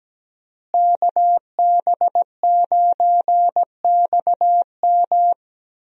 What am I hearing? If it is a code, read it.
Morse code, heard as a keyed tone, about 17 wpm, 700 Hz. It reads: KB9XM
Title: USA Call Signs - 5 Characters - Repeated 3x - 17wpm